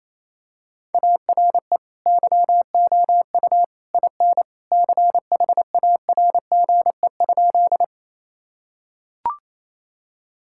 Morse code, heard as a keyed tone, about 28 wpm, 700 Hz. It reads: ARE YOU IN CHARGE?
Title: Sentences from Top 1000 Words - Review Part 6 - 28wpm